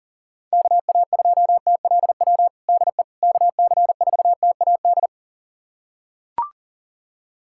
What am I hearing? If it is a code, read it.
Morse code, heard as a keyed tone, about 40 wpm, 690 Hz. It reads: KA2TLW DE KC4TAD
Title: USA Callsign DE Callsign - 6 Characters - 40wpm